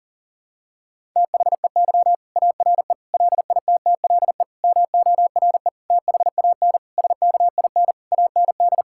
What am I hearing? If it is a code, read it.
Morse code, heard as a keyed tone, about 40 wpm, 700 Hz. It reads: THEY ARE LITTLE MORE THAN SKIN AND
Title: Sentences from Top 1600 Words 40wpm